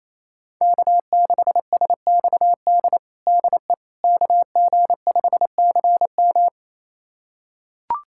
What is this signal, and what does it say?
Morse code, heard as a keyed tone, about 28 wpm, 700 Hz. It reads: K6SXD DE KG5CM
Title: USA Callsign DE Callsign - 5 Characters - 28wpm